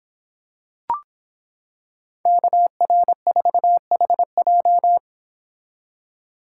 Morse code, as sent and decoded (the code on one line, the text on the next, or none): -.- .-. ....- .... .---
KR4HJ